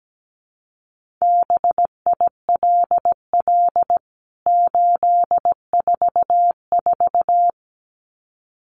Morse code, heard as a keyed tone, about 17 wpm, 700 Hz. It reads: BILL 844
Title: CWT Exchange 17wpm